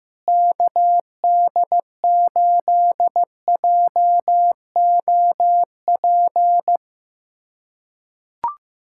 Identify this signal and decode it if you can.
Morse code, heard as a keyed tone, about 15 wpm, 700 Hz. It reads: KD8JOP
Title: USA Callsign DE Callsign - 6 Characters - 15wpm